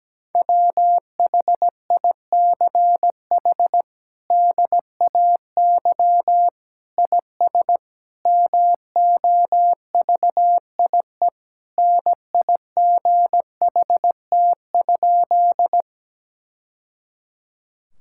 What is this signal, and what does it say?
Morse code, heard as a keyed tone, about 17 wpm, 700 Hz. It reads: WHICH DAY IS MOVIE NIGHT?
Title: Sentences from Top 1000 Words - Review Part 5 - 17wpm